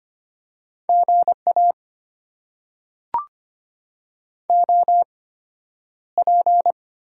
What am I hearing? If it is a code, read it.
Morse code, heard as a keyed tone, about 25 wpm, spaced as for 11 wpm, 700 Hz. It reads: GA OP
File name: Top 90 QSO Elements - 8x Character Spacing - 25wpm